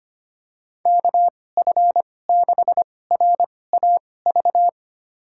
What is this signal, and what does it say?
Morse code, heard as a keyed tone, about 25 wpm, spaced as for 18 wpm, 700 Hz. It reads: KF6RAV
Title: USA Call Signs - 6 Characters - 2x Character Spacing - 25wpm